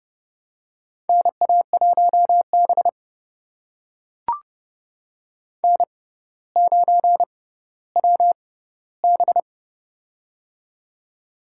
Morse code, heard as a keyed tone, about 30 wpm, spaced as for 15 wpm, 700 Hz. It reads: NA1B N9WB
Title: USA Call Signs - 4 Characters - 6x Character Spacing - 30wpm